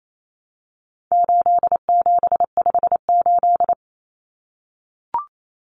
Morse code, heard as a keyed tone, about 28 wpm, 700 Hz. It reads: 8758
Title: Numbers - 4-Digit 28wpm